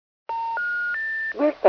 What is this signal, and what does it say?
Special information tone, with speech.